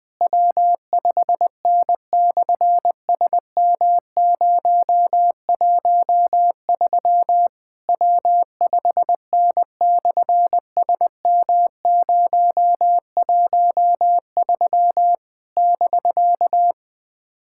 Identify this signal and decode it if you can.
Morse code, heard as a keyed tone, about 19 wpm, 700 Hz. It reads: W5N/SM013 W5N/SM013 <BK>
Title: SOTA QSO Practice 20wpm